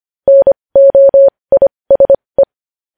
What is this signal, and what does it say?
Morse code, heard as a keyed tone, about 25 wpm, spaced as for 20 wpm, 550 Hz. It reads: NOISE